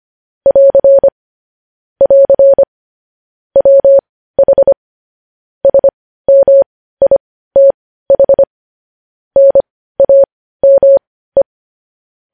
Morse code, heard as a keyed tone, about 25 wpm, spaced as for 15 wpm, 550 Hz. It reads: <AR> <AR> WH SMITH NAME
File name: podcast / News headlines-1x-25x15